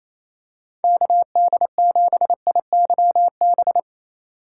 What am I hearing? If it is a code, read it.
Morse code, heard as a keyed tone, about 28 wpm, 700 Hz. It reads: KD7IYB
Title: USA Call Signs - 6 Characters - 2x Character Spacing - 28wpm